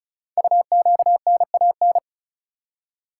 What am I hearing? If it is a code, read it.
Morse code, heard as a keyed tone, about 36 wpm, 700 Hz. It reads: UQNAN
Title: Code Groups - 5 Letters - 35wpm